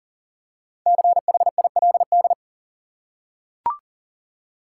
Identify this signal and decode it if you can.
Morse code, heard as a keyed tone, about 40 wpm, 700 Hz. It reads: CHILD
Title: Five Letter Words - 40wpm